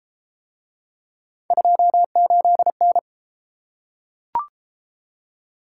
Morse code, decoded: 28N

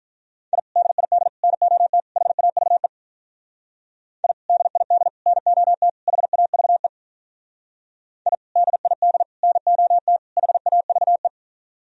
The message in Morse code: I DID NOT HAVE I DID NOT HAVE I DID NOT HAVE

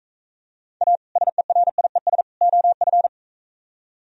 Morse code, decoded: A SERIES OF

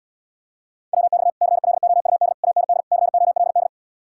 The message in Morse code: WX CLOUD AND COLD

